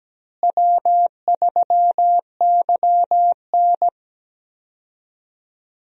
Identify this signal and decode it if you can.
Morse code, heard as a keyed tone, about 17 wpm, 700 Hz. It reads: W3YN
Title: USA Call Signs - 4 Characters - Repeated 3x - 17wpm